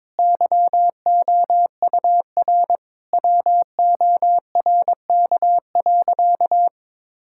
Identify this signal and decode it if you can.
Morse code, heard as a keyed tone, about 22 wpm, 700 Hz. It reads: YOUR WORK.
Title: Sentences from Top 1000 Words - Review Part 6 - 22wpm